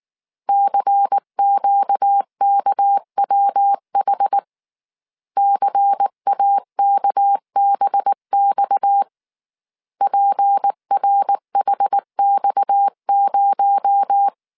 Morse code, heard as a keyed tone, about 19 wpm, 800 Hz. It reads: CQKWH CAKBX PRHX0
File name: CW / DET / 1 6 1500C S C 18wpm 2356